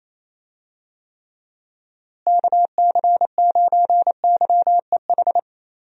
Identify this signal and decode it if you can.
Morse code, heard as a keyed tone, about 27 wpm, 700 Hz. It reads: KC9YEH
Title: USA Call Signs - 6 Characters - Repeated 2x - 28wpm